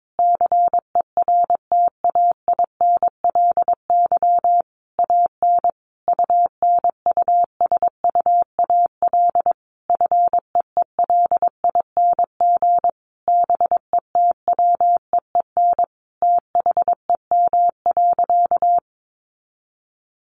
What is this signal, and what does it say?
Morse code, heard as a keyed tone, about 22 wpm, 700 Hz. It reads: CERTAINLY AN UNUSUAL FEELING BETWEEN THEM.
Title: Sentences from Top 1400 Words 22wpm